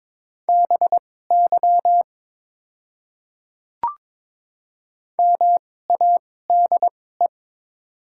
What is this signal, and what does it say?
Morse code, heard as a keyed tone, about 22 wpm, spaced as for 16 wpm, 700 Hz. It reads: BY MADE
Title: 2x Character Spacing - Top 200 Words - 22wpm